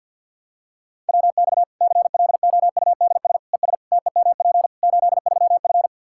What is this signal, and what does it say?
Morse code, heard as a keyed tone, about 52 wpm, 700 Hz. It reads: WX CLOUDS ES TEMP 83F
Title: Practice QSO - Exchange 2 - Info - 50wpm